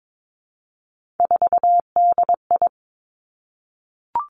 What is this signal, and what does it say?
Morse code, heard as a keyed tone, about 22 wpm, 700 Hz. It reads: 4DI
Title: Three Challenging Letter-Number Combinations 22wpm